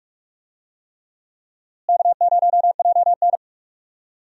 Morse code, heard as a keyed tone, about 47 wpm, 700 Hz. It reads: K0JN
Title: USA Call Signs - 4 Characters - Repeated 3x - 45wpm